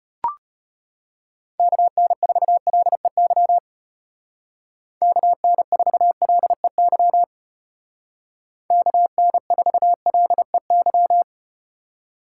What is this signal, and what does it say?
Morse code, heard as a keyed tone, about 34 wpm, 700 Hz. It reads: KN4LEY KN4LEY KN4LEY